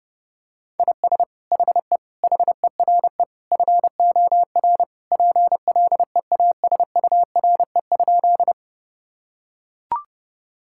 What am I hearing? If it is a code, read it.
Morse code, heard as a keyed tone, about 32 wpm, 700 Hz. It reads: IS HE HERE FOR PLEASURE?
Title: Sentences from Top 1600 Words 30wpm